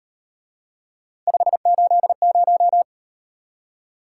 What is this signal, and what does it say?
Morse code, heard as a keyed tone, about 37 wpm, 700 Hz. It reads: 580